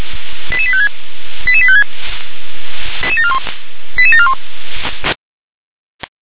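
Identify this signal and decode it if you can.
Five-tone selective call (ZVEI), heard as quick runs of five tones, tones of about 70 ms. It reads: 88856 88856 88741
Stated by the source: ZVEI five-tone selective call received with an SDR.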